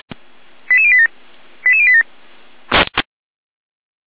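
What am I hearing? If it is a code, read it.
Five-tone selective call (ZVEI), heard as quick runs of five tones, tones of about 70 ms. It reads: 88087 88087
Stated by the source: Five-tone ZVEI selective call received by SDR.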